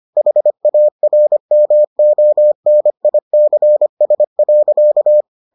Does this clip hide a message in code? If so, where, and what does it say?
Yes: Morse code from 0.2 to 5.2 s, heard as a keyed tone, about 25 wpm, 600 Hz. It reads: HARMONICS.